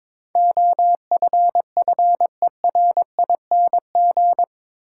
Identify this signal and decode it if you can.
Morse code, heard as a keyed tone, about 22 wpm, 700 Hz. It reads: OFFERING
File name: practice / 22wpm / Eight Letter Words - 22wpm